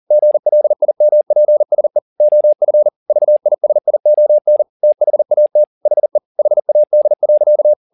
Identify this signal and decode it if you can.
Morse code, heard as a keyed tone, about 40 wpm, 600 Hz. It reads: GLIMPSE OF VISION THAT HE HAD.
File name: the old man and the sea 2 40wpm